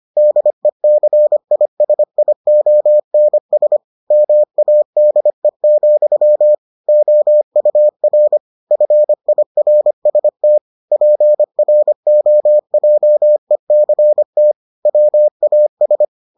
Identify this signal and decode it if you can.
Morse code, heard as a keyed tone, about 25 wpm, 600 Hz. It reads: DECISIONS MADE, OUR FIRST PROJECT WAS